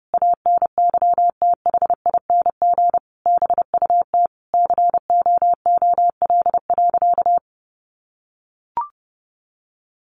Morse code, heard as a keyed tone, about 30 wpm, 700 Hz. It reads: ANYTHING BUT COOL.
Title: Sentences from Top 1200 Words 30wpm